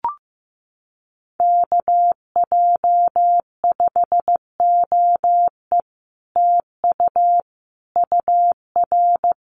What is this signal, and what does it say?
Morse code, heard as a keyed tone, about 15 wpm, 700 Hz. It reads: KJ5OE TU UR
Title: Common QSO Phrases Encore 15wpm